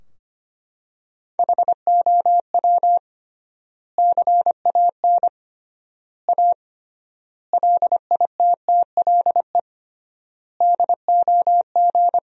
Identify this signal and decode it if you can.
Morse code, heard as a keyed tone, about 25 wpm, 700 Hz. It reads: HOW CAN A LITTLE DOG
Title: 3x Word-Spacing - Sentences from Top 500 Words - 25wpm